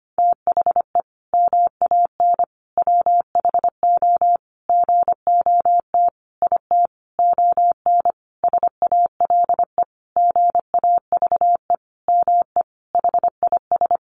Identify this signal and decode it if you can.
Morse code, heard as a keyed tone, about 25 wpm, 700 Hz. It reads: THE MAN WHO GOT IT ON SALE GAVE ME HIS
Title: Sentences from Top 1400 Words 25wpm